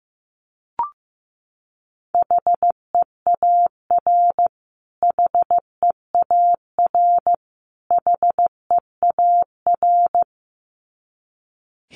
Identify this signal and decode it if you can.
Morse code, heard as a keyed tone, about 15 wpm, 700 Hz. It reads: HEAR HEAR HEAR
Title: Top 500 Words Repeated 15wpm